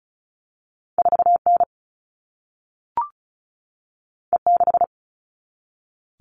Morse code, heard as a keyed tone, about 36 wpm, 700 Hz. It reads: 4N E6
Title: All Cut Numbers - 2-Digit 35wpm